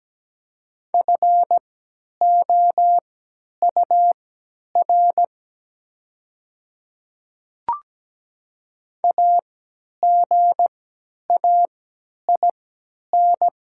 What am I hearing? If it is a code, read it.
Morse code, heard as a keyed tone, about 17 wpm, spaced as for 10 wpm, 700 Hz. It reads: FOUR AGAIN
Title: 3x Character Spacing - Top 500 Words - 17wpm